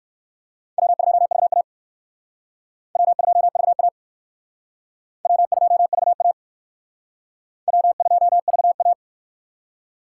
Morse code, heard as a keyed tone, about 59 wpm, 700 Hz. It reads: W2VA W2VA W2VA W2VA